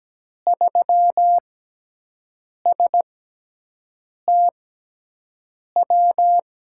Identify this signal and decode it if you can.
Morse code, heard as a keyed tone, about 17 wpm, spaced as for 6 wpm, 700 Hz. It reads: 3STW
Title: USA Call Signs - 6x Character Spacing - 17wpm